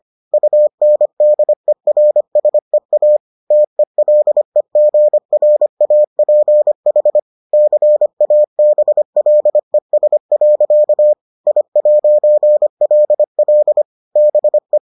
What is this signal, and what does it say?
Morse code, heard as a keyed tone, about 25 wpm, 600 Hz. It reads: UNDERSEA TELEGRAPH CABLES. I'LL BE